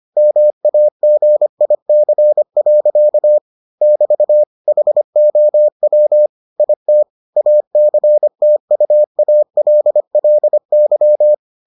Morse code, heard as a keyed tone, about 25 wpm, 600 Hz. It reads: MAGIC. <BT> HOW IT ACTUALLY